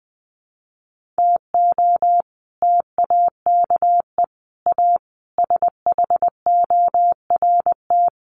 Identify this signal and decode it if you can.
Morse code, heard as a keyed tone, about 20 wpm, 700 Hz. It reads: TO TAKE A SHORT
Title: Sets of 5 Words Encore2 20wpm